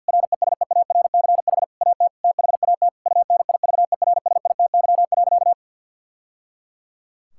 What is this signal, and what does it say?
Morse code, heard as a keyed tone, about 50 wpm, 700 Hz. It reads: RESEARCH AT THAT UNIVERSITY.
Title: Sentences from Top 1000 Words - Review Part 6 - 50wpm